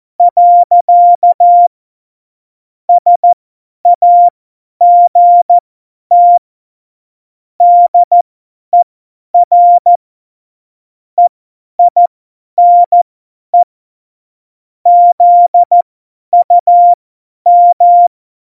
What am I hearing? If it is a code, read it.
Morse code, heard as a keyed tone, about 14 wpm, spaced as for 10 wpm, 700 Hz. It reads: . SAGT DER EINE ZUM
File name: Morsekurs / Witze / Witz 11. September 2018, 50 F70